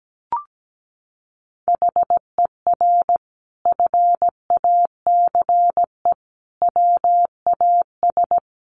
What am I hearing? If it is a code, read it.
Morse code, heard as a keyed tone, about 17 wpm, 700 Hz. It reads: HER FACE WAS
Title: Sentences from Top 1200 Words 17wpm